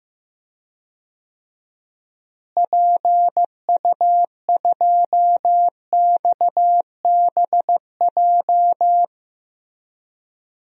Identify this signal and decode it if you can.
Morse code, heard as a keyed tone, about 15 wpm, 700 Hz. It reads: PU2XBJ
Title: DXCC-100 Call Signs - 6 Characters - Repeated 2x - 15wpm